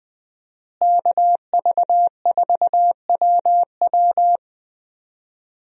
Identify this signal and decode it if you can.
Morse code, heard as a keyed tone, about 20 wpm, 700 Hz. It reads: KV4WW
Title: USA Call Signs - 5 Characters - Repeated 2x - 20wpm